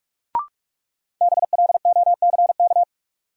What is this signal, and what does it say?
Morse code, heard as a keyed tone, about 46 wpm, 700 Hz. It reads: BLOCK
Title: Sentences from Top 1600 Words - New Word Drill Alt - 45wpm